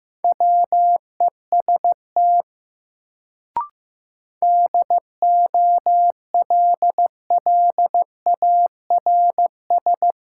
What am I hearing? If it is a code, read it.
Morse code, heard as a keyed tone, about 15 wpm, 700 Hz. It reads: WEST DOLLARS
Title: Sets of 5 Words - New Word Drill - 15wpm